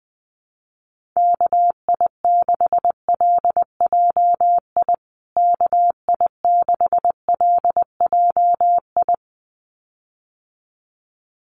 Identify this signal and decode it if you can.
Morse code, heard as a keyed tone, about 20 wpm, 700 Hz. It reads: KI6LJI KI6LJI
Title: USA Call Signs Repeated 2x 20wpm